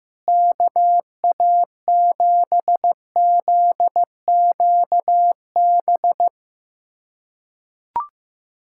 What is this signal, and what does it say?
Morse code, heard as a keyed tone, about 16 wpm, 700 Hz. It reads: KA7ZQB
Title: USA Callsign DE Callsign - 6 Characters - 15wpm